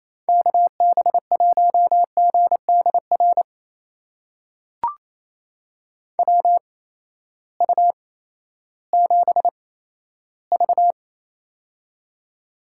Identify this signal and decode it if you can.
Morse code, heard as a keyed tone, about 28 wpm, spaced as for 14 wpm, 700 Hz. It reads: KB1GDR WU7V
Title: USA Call Signs - 8x Character Spacing - 28wpm